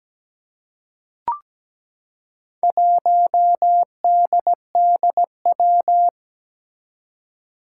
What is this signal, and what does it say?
Morse code, heard as a keyed tone, about 17 wpm, 700 Hz. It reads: 1DDW
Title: Four Challenging Letter-Number Combinations 17wpm